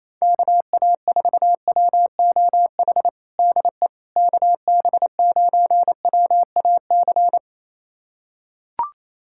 Morse code, heard as a keyed tone, about 28 wpm, 700 Hz. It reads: KA4WOH DE KB9WAC
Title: USA Callsign DE Callsign 28wpm